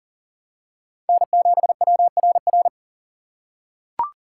Morse code, heard as a keyed tone, about 40 wpm, 700 Hz. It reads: N7WRR
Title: USA Call Signs - 5 Characters - 4x Character Spacing - 40wpm